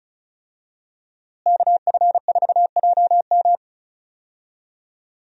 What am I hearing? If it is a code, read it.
Morse code, heard as a keyed tone, about 34 wpm, 700 Hz. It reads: KF4JM
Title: USA Call Signs - 5 Characters - Repeated 3x - 35wpm